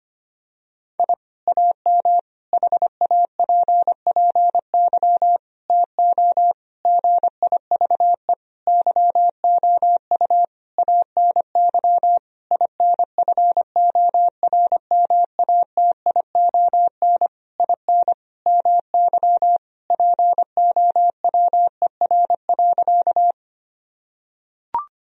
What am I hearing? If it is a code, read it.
Morse code, heard as a keyed tone, about 25 wpm, 700 Hz. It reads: I AM HAPPY TO GIVE YOU ANY INFORMATION IN MY POWER.
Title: Sentences from Top 1000 Words - Review Part 3 - 25wpm